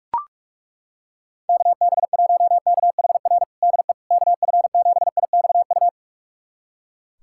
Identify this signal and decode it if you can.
Morse code, heard as a keyed tone, about 42 wpm, 690 Hz. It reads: KB1KHR DE KF7IXU
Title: USA Callsign DE Callsign - 6 Characters - 45wpm